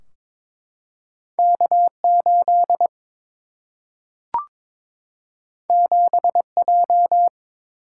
Morse code, heard as a keyed tone, about 22 wpm, 700 Hz. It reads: K 8 7 J
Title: Single-Letter Single-Number 22wpm